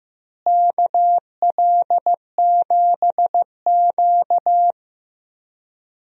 Morse code, heard as a keyed tone, about 15 wpm, 700 Hz. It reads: KL7Q
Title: DXCC-100 Call Signs - 4 Characters - Repeated 3x - 15wpm